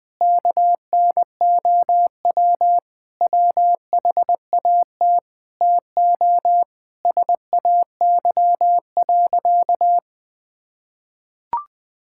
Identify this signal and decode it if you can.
Morse code, heard as a keyed tone, about 21 wpm, 700 Hz. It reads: KNOW WHAT TO SAY.
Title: Sentences with 5-Word Phrases - Spoken Beginning - 20wpm